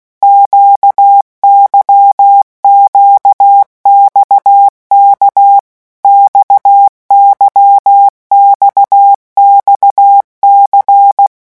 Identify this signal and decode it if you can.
Morse code, heard as a keyed tone, about 16 wpm, 800 Hz. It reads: QYQXK XYXXC
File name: morse mp3 / revision1-vitesse 15 mots